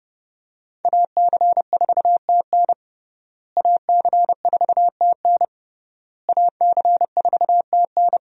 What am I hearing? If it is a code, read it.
Morse code, heard as a keyed tone, about 32 wpm, 700 Hz. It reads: AC4TN AC4TN AC4TN
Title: USA Call Signs - 5 Characters - Repeated 3x - 30wpm